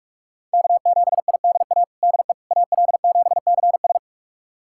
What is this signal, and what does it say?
Morse code, heard as a keyed tone, about 46 wpm, 700 Hz. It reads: K7IDA DE AL7CS
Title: USA Callsign DE Callsign - 5 Characters - 45wpm